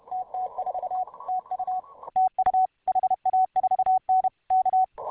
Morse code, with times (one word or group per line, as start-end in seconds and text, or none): 0.1-1.0 TT4
1.3-1.8 TU
2.2-2.7 TU
2.9-4.3 HA4N
4.5-4.8 K